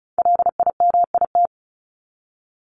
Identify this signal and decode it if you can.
Morse code, heard as a keyed tone, about 34 wpm, 700 Hz. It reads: LIMIT
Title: Top 1251-1500 Words 35wpm